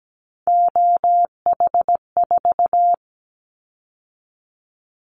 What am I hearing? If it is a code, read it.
Morse code, heard as a keyed tone, about 17 wpm, 700 Hz. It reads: OH4
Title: Callsign Prefixes for Top 50 DXCC - 17wpm